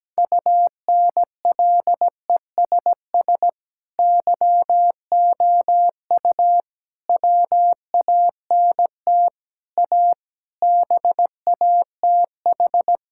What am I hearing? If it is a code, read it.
Morse code, heard as a keyed tone, about 17 wpm, 700 Hz. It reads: UNLESS YOU WANT A BATH
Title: Sentences from Top 1600 Words 17wpm